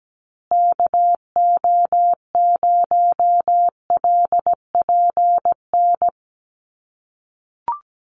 Morse code, heard as a keyed tone, about 17 wpm, 700 Hz. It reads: KO0LPN
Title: USA Call Signs Repeated 3x 17wpm